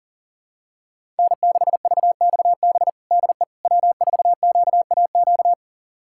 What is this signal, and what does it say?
Morse code, heard as a keyed tone, about 40 wpm, 700 Hz. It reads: N6VXB DE W4QAQ
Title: USA Callsign DE Callsign - 5 Characters - 40wpm